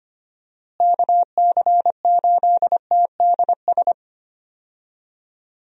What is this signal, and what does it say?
Morse code, heard as a keyed tone, about 25 wpm, 700 Hz. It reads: KC8TDS
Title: USA Call Signs - 6 Characters - Repeated 3x - 25wpm